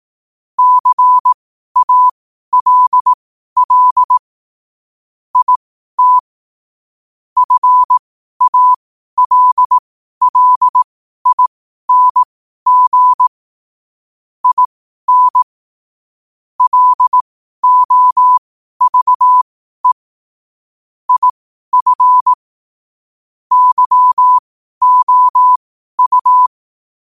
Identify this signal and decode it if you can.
Morse code, heard as a keyed tone, about 18 wpm, spaced as for 12 wpm, 1000 Hz. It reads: CALL IT FALLING IN LOVE IF YOU
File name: qotdfiles / 12wpm